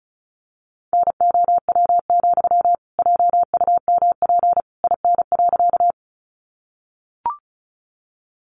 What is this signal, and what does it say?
Morse code, heard as a keyed tone, about 34 wpm, 700 Hz. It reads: NOW, JUMP IN.
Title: Sentences from Top 1300 Words 35wpm